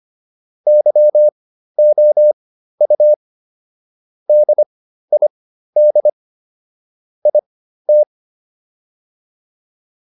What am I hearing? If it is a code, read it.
Morse code, heard as a keyed tone, about 25 wpm, spaced as for 13 wpm, 600 Hz. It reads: YOU DID IT
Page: intermediate-practice-files